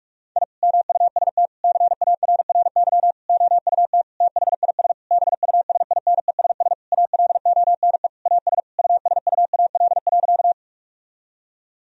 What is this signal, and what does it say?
Morse code, heard as a keyed tone, about 46 wpm, 700 Hz. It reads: I MUST CARRY OUT THIS BUSINESS ALONE AS USUAL.
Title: Sentences from Top 1000 Words - Review Part 2 - 45wpm